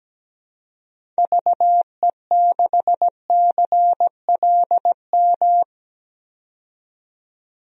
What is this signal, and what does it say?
Morse code, heard as a keyed tone, about 17 wpm, 700 Hz. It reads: VE6CLM
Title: DXCC-100 Call Signs - 6 Characters - 17wpm